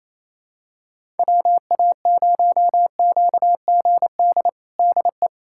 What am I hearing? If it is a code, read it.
Morse code, heard as a keyed tone, about 28 wpm, 700 Hz. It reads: WA0QGD DE
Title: USA Callsign DE Callsign 28wpm